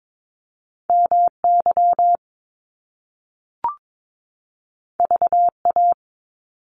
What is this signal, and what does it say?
Morse code, heard as a keyed tone, about 22 wpm, 700 Hz. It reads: MY VA
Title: Top Two Letter Combinations in Words 22wpm